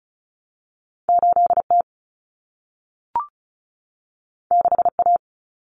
Morse code, heard as a keyed tone, about 36 wpm, 700 Hz. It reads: 8T 6A
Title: Common Cut Numbers - 2-Digit 35wpm